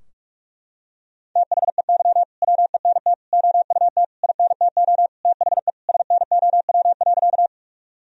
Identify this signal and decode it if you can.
Morse code, heard as a keyed tone, about 41 wpm, 700 Hz. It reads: THEY WENT OUT INTO THE SNOW.
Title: Sentences from Top 1000 Words - Review Part 3 - 45wpm